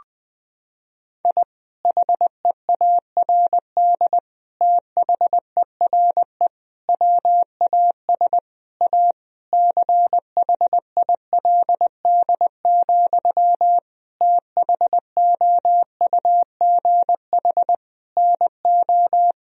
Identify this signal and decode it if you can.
Morse code, heard as a keyed tone, about 21 wpm, 700 Hz. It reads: I HEARD THERE WAS A CHILD, THOUGH NO
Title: Sentences from Top 1000 Words - Review Part 2 - 20wpm